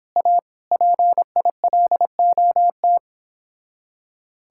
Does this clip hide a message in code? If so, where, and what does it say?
Yes: Morse code from 0.2 to 3.0 s, heard as a keyed tone, about 29 wpm, 700 Hz. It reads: A PILOT